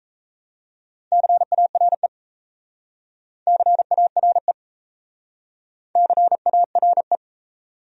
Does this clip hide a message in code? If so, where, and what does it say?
Yes: Morse code from 1.1 to 7.2 s, heard as a keyed tone, about 36 wpm, 700 Hz. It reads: CARE CARE CARE